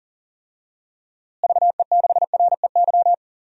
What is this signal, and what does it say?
Morse code, heard as a keyed tone, about 39 wpm, 700 Hz. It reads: VE6REY
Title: DXCC-100 Call Signs - 6 Characters - Repeated 3x - 40wpm